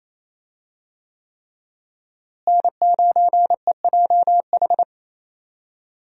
Morse code, heard as a keyed tone, about 28 wpm, 700 Hz. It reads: N9EJH
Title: USA Call Signs - 5 Characters - Repeated 3x - 28wpm